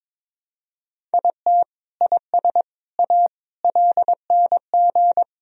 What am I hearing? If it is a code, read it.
Morse code, heard as a keyed tone, about 22 wpm, 700 Hz. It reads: IT IS A LNG
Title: Sentences from Top 300 Words and Abbreviations 22wpm